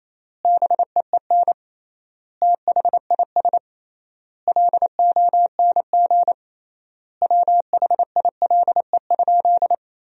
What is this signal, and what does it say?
Morse code, heard as a keyed tone, about 28 wpm, 700 Hz. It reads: BEEN THIS LONG WHILE?
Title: 3x Word-Spacing - Sentences from Top 300 Words - 28wpm